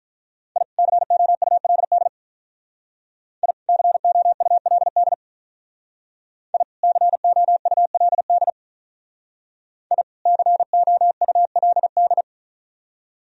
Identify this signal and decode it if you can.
Morse code, heard as a keyed tone, about 45 wpm, 700 Hz. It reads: I COULD I COULD I COULD I COULD